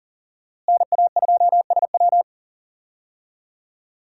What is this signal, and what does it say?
Morse code, heard as a keyed tone, about 40 wpm, 700 Hz. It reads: NA2SW
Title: USA Call Signs - 5 Characters - Repeated 3x - 40wpm